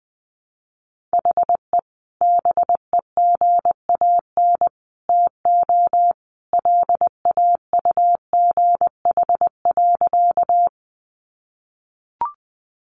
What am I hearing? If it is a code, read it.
Morse code, heard as a keyed tone, about 20 wpm, 700 Hz. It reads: HE BEGAN TO LAUGH.
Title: Sentences from Top 500 Words 20wpm